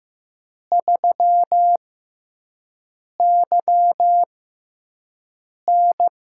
Morse code, heard as a keyed tone, about 15 wpm, spaced as for 5 wpm, 700 Hz. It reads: 3YN
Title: USA Call Signs - 4 Characters - 6x Character Spacing - 15wpm